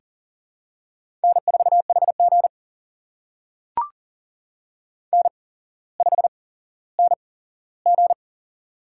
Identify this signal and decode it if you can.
Morse code, heard as a keyed tone, about 40 wpm, spaced as for 17 wpm, 700 Hz. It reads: N4HG N5NG
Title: USA Call Signs - 4 Characters - 8x Character Spacing - 40wpm